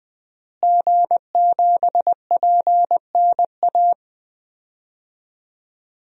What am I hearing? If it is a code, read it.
Morse code, heard as a keyed tone, about 21 wpm, 700 Hz. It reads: G7PNA
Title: DXCC-100 Call Signs - 5 Characters - Repeated 3x - 20wpm